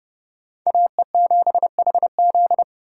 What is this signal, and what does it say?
Morse code, heard as a keyed tone, about 30 wpm, 700 Hz. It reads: AE7HZ
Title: USA Call Signs - 5 Characters - 6x Character Spacing - 30wpm